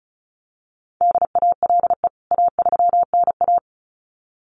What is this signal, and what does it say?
Morse code, heard as a keyed tone, about 34 wpm, 700 Hz. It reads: DALE A3NA
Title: CWT Exchange 35wpm